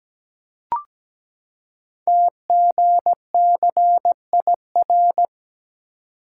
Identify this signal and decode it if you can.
Morse code, heard as a keyed tone, about 17 wpm, 700 Hz. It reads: TGCIR
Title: Code Groups - 5 Letters - 17wpm